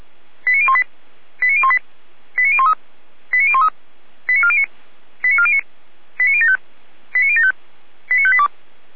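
Five-tone selective call (ZVEI), heard as quick runs of five tones, tones of about 70 ms. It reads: 89018 89018 89013 89013 89409 89409 89075 89075 89582
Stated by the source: ZVEI five-tone selective call received on an SDR.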